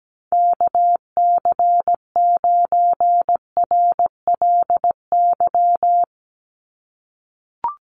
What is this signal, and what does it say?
Morse code, heard as a keyed tone, about 17 wpm, 700 Hz. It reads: KC9RLY
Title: USA Call Signs 17wpm